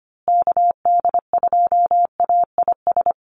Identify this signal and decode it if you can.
Morse code, heard as a keyed tone, about 25 wpm, 700 Hz. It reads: KD2AIS
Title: USA Call Signs Repeated 2x 25wpm